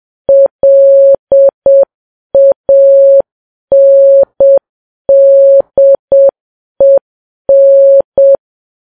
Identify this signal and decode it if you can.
Morse code, heard as a keyed tone, about 7 wpm, 550 Hz. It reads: LANDEN
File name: Files Web tls / Morse on9fv / morse7kps